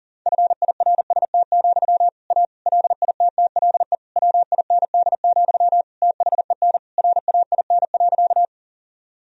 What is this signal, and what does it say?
Morse code, heard as a keyed tone, about 40 wpm, 700 Hz. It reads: FIRST, A LITTLE WIND, THEN RAIN.
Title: Sentences from Top 1000 Words - Review Part 2 - 40wpm